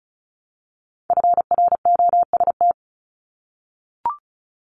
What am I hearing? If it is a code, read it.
Morse code, heard as a keyed tone, about 36 wpm, 700 Hz. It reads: FROST
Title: Multi-letter Phonemes Repeated 3x 35wpm